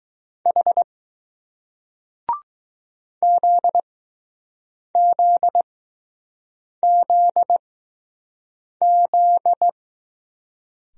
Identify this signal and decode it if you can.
Morse code, heard as a keyed tone, about 20 wpm, spaced as for 6 wpm, 700 Hz. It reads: H ZZZZ